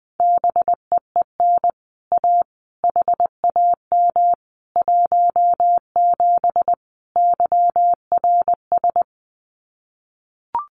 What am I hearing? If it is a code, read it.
Morse code, heard as a keyed tone, about 20 wpm, 700 Hz. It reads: BEEN A HAM 17 YRS
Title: Common QSO Phrases Encore 20wpm